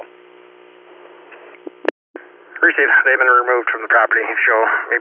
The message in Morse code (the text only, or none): none